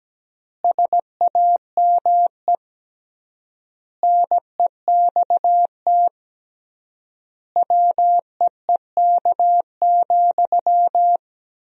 Morse code, heard as a keyed tone, about 17 wpm, 700 Hz. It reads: SAME NEXT WEEK,
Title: 3x Word-Spacing - Sentences from Top 500 Words - 17wpm